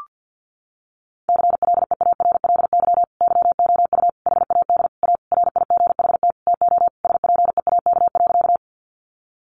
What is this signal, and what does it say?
Morse code, heard as a keyed tone, about 50 wpm, 700 Hz. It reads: CLEARLY YOU HAD A RIGHT TO SPEAK.
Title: Sentences from Top 1300 Words 50wpm